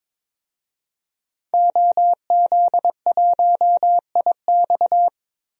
Morse code, heard as a keyed tone, about 22 wpm, 700 Hz. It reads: OZ1IX
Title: DXCC-100 Call Signs - 5 Characters - 22wpm